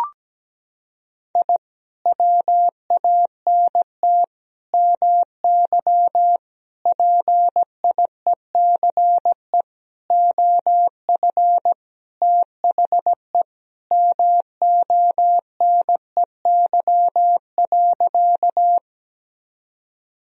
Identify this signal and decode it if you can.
Morse code, heard as a keyed tone, about 17 wpm, 700 Hz. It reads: I WANT MY PIECE OF THE MONEY.
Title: Sentences from Top 1000 Words - Review Part 2 - 17wpm